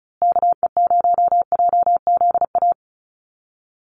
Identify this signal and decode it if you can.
Morse code, heard as a keyed tone, about 34 wpm, 700 Hz. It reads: KE0JZA
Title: USA Call Signs Repeated 2x 35wpm